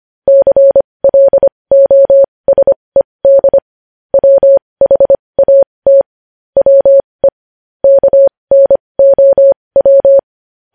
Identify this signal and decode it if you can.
Morse code, heard as a keyed tone, about 25 wpm, spaced as for 20 wpm, 550 Hz. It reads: CLOSED WHAT WE KNOW
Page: news-headlines-in-morse-code-at-20-wpm